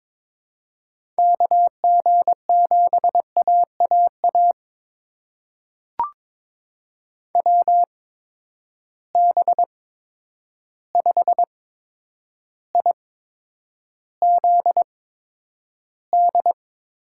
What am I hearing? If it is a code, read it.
Morse code, heard as a keyed tone, about 22 wpm, spaced as for 9 wpm, 700 Hz. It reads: KG7AAA WB5IZD
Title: USA Call Signs - 6 Characters - 8x Character Spacing - 22wpm